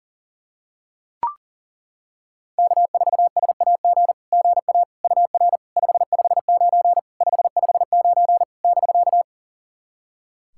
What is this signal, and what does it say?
Morse code, heard as a keyed tone, about 40 wpm, 700 Hz. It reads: K4SAG GA UR 559 559 <BK>
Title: SOTA QSO Practice 40wpm